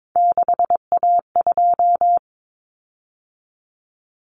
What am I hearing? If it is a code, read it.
Morse code, heard as a keyed tone, about 22 wpm, 700 Hz. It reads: 6A2
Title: Common Cut Numbers - 3-Digit 22wpm